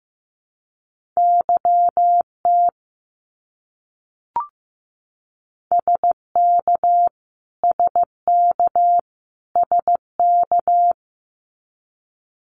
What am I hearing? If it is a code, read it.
Morse code, heard as a keyed tone, about 15 wpm, 700 Hz. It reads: YT SK SK SK
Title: Canadian Provinces and Territories - Repeated 3x - 15wpm